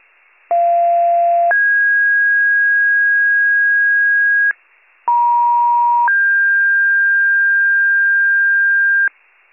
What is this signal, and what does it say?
Two-tone page, heard as a one-second tone, then a three-second tone, about 690 then 1650 Hz, about 950 then 1600 Hz. It is heard from 0.5 to 9.1 s.